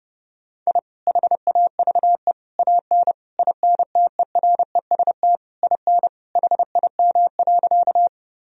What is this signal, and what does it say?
Morse code, heard as a keyed tone, about 30 wpm, 700 Hz. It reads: I HAVE AN INTEREST IN HIM.
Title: Sentences from Top 1000 Words - Review Part 2 - 30wpm